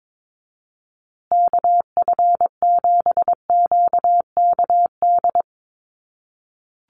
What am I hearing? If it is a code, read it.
Morse code, heard as a keyed tone, about 22 wpm, 700 Hz. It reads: KF7QKD
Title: USA Call Signs Repeated 2x 22wpm